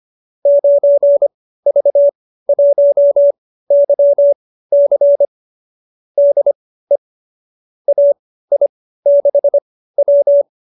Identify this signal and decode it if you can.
Morse code, heard as a keyed tone, about 25 wpm, spaced as for 15 wpm, 590 Hz. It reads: 9V1YC DE AI6W